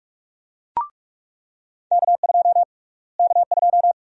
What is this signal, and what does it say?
Morse code, heard as a keyed tone, about 46 wpm, 700 Hz. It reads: K2 K2
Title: USA Call Sign Prefixes - Repeated 2x - 45wpm